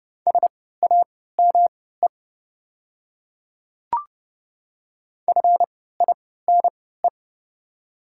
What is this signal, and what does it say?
Morse code, heard as a keyed tone, about 31 wpm, spaced as for 17 wpm, 690 Hz. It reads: SAME FINE
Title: 3x Character Spacing - Top 500 Words - 30wpm